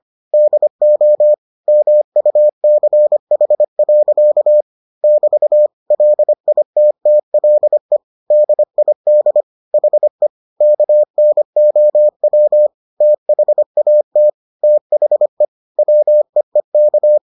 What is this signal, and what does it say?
Morse code, heard as a keyed tone, about 25 wpm, 600 Hz. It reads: DO MUCH. <BT> LITTLE DID HE KNOW THAT THE WEEK